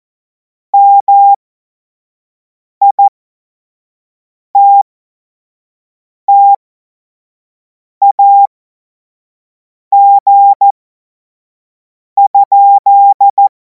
Morse code, heard as a keyed tone, about 14 wpm, spaced as for 5 wpm, 800 Hz. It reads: MITTAG?
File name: Morsekurs / Witze / Witz 17. Februar 2026 25 F70